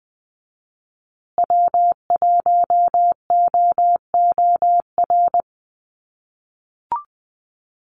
Morse code, heard as a keyed tone, about 20 wpm, 700 Hz. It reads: W1OOR
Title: USA Call Signs Repeated 2x 20wpm